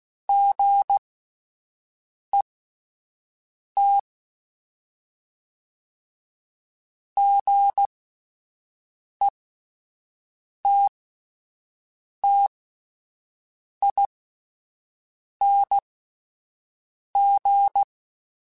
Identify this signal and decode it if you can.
Morse code, heard as a keyed tone, about 16 wpm, spaced as for 5 wpm, 800 Hz. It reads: GET GETTING
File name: Short Runs / Short Runs-mp3s / Gotten @10